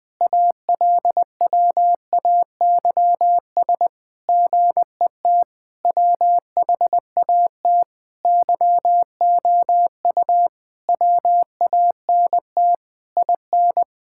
Morse code, heard as a keyed tone, about 20 wpm, 700 Hz. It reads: ALWAYS GET WHAT YOU WANT IN
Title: Sentences with 7-Word Phrases - Spoken Beginning - 20wpm